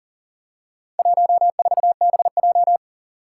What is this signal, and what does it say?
Morse code, heard as a keyed tone, about 40 wpm, 700 Hz. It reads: 14BJ
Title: Four Challenging Letter-Number Combinations 40wpm